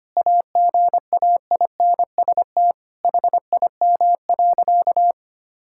Morse code, heard as a keyed tone, about 25 wpm, 700 Hz. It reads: AGAINST HIM.
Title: Sentences from Top 1600 Words 25wpm